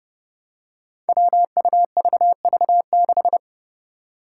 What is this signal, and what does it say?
Morse code, heard as a keyed tone, about 31 wpm, 700 Hz. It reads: WUVV6
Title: Five Challenging Letter-Number Combinations 30wpm